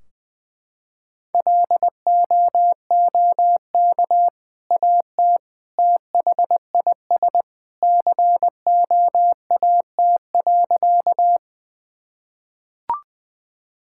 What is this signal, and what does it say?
Morse code, heard as a keyed tone, about 19 wpm, 700 Hz. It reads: LOOK AT THIS COAT.
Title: Sentences from Top 1600 Words 20wpm